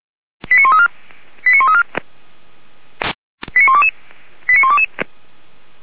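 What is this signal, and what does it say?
Five-tone selective call (ZVEI), heard as quick runs of five tones, tones of about 70 ms. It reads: 89134 89134 89122 89122
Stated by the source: ZVEI five-tone selective call received by SDR.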